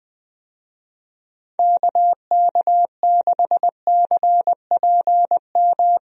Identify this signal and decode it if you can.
Morse code, heard as a keyed tone, about 20 wpm, 700 Hz. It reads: KK6CPM
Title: USA Call Signs - 6 Characters - 20wpm